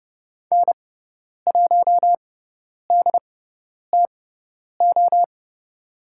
Morse code, heard as a keyed tone, about 30 wpm, spaced as for 10 wpm, 700 Hz. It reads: N1DTO
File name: Lesson 16g - Letter c - Call Signs 30wpm